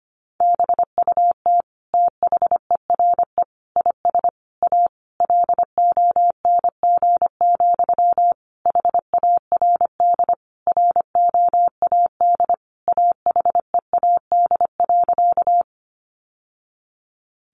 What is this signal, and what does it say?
Morse code, heard as a keyed tone, about 25 wpm, 700 Hz. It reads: BUT THERE IS A LONG, HARD ROAD AHEAD.
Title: Sentences from Top 1100 Words 25wpm